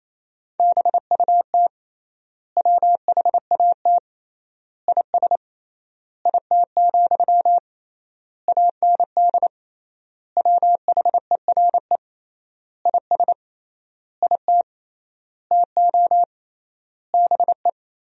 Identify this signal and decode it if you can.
Morse code, heard as a keyed tone, about 28 wpm, 700 Hz. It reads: BUT WHAT IS IT, AND WHERE IS IT TO BE
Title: 3x Word-Spacing - Sentences from Top 300 Words - 28wpm